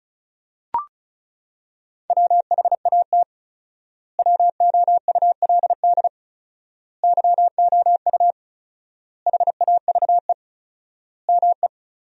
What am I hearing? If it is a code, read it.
Morse code, heard as a keyed tone, about 34 wpm, 700 Hz. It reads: WHAT WOULD YOU HAVE ME
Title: 4x Word-Spacing - Sentences from Top 200 Words - 35wpm